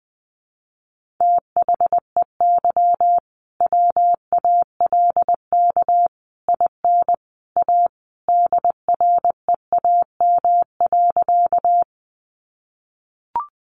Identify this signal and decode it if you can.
Morse code, heard as a keyed tone, about 20 wpm, 700 Hz. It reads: THEY WALK IN A DREAM.
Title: Sentences from Top 1500 Words 20wpm